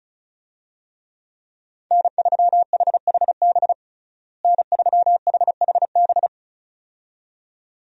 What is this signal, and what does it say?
Morse code, heard as a keyed tone, about 36 wpm, 700 Hz. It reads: N3HHB N3HHB
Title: USA Call Signs - 5 Characters - Repeated 2x - 35wpm